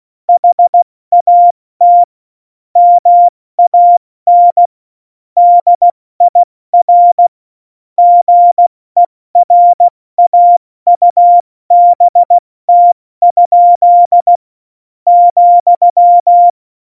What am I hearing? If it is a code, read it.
Morse code, heard as a keyed tone, about 16 wpm, 700 Hz. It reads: HAT MAN DIR GERAUBT? ,
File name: Morsekurs / Witze / Witz 26. Februar 2019, 70 F80